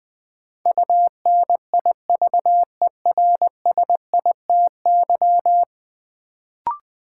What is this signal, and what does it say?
Morse code, heard as a keyed tone, about 20 wpm, 700 Hz. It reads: UNIVERSITY
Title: Sentences from Top 1000 Words - Review Part 4 - New Word Drill - 20wpm